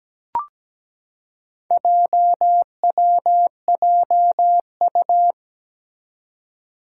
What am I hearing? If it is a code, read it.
Morse code, heard as a keyed tone, about 17 wpm, 700 Hz. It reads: JWJU
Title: Four Challenging Letter-Number Combinations 17wpm